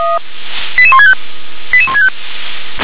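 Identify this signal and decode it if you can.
Five-tone selective call (ZVEI), heard as quick runs of five tones, tones of about 80 ms. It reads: 88165 88165
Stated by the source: Five-tone ZVEI selective call received on an SDR.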